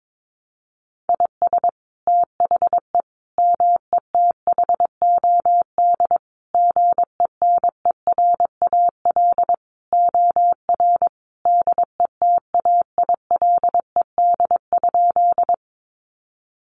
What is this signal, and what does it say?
Morse code, heard as a keyed tone, about 22 wpm, 700 Hz. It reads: IS THE METHOD GENERAL OR DETAILED?
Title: Sentences from Top 1500 Words 22wpm